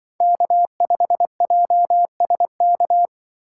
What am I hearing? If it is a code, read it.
Morse code, heard as a keyed tone, about 23 wpm, 700 Hz. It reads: K5JSK
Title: Speed Racing - USA 1x3 Call Signs 20wpm